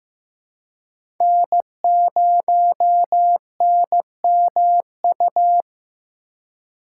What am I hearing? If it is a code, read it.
Morse code, heard as a keyed tone, about 15 wpm, 700 Hz. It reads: N0NMU
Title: USA Call Signs - 3x Character Spacing - 15wpm